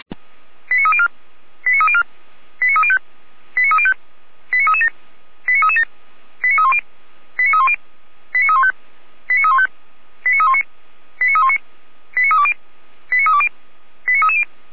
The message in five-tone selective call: 89304 89304 89305 89305 89337 89337 89310 89310 89315 89315 89319 89319 89320 89320 89330

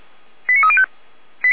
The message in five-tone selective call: 89305

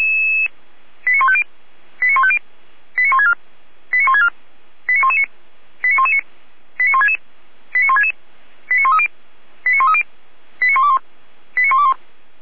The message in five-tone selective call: 89150 89150 89164 89164 89119 89119 89166 89166 89130 89130 89121 89121

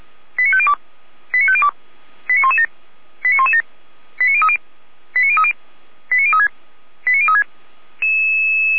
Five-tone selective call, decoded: 89502 89502 89117 89117 89030 89030 89036 89036

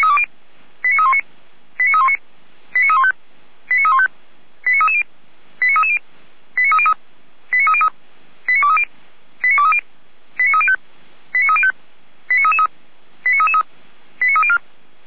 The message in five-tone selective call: 89319 89319 89315 89315 89330 89330 89303 89303 89230 89230 89305 89305 89333 89333 89304

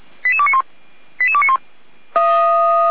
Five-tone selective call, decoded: 88291 88291

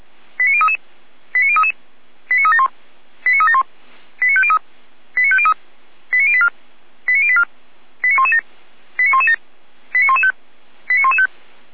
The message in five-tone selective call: 89033 89033 89371 89371 89503 89503 89084 89084 89117 89117 89115 89115

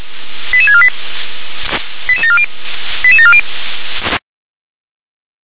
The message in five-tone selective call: 88648 88740 88740

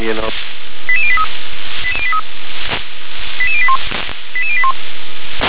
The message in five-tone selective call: 88083 88081 88081

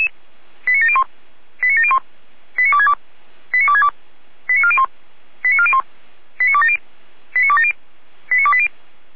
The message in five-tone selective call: 89701 89701 89262 89262 89401 89401 89270 89270 89280

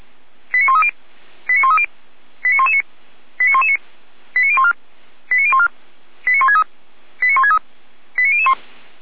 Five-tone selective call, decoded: 89130 89130 89119 89119 89014 89014 89163 89163 89001

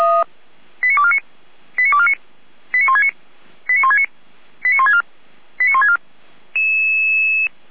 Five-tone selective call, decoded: 89249 89249 89169 89169 89164 89164

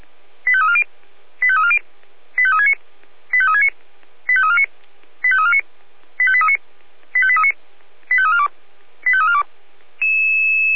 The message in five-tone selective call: 85340 85340 85369 85369 85349 85349 85839 85839 85342 85342

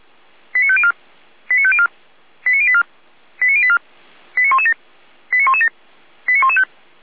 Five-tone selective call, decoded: 89504 89504 89084 89084 89117 89117 89115

